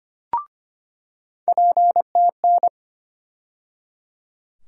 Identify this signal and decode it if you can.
Morse code, heard as a keyed tone, about 25 wpm, 700 Hz. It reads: PTN
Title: USA Call Sign Postfixes - 3 Letters 25wpm